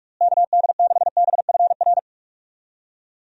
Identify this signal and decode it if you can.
Morse code, heard as a keyed tone, about 46 wpm, 700 Hz. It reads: KD6BFR
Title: USA Call Signs - 6 Characters - Repeated 2x - 45wpm